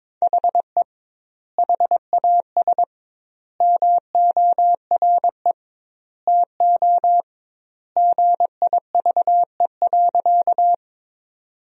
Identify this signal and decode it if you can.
Morse code, heard as a keyed tone, about 21 wpm, 700 Hz. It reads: HE HAS MORE TO GIVE.
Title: 2x Word-Spacing - Sentences from Top 200 Words - 22wpm